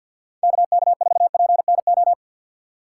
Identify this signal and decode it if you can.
Morse code, heard as a keyed tone, about 53 wpm, 700 Hz. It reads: KK4PNO
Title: USA Call Signs - 6 Characters - 3x Character Spacing - 50wpm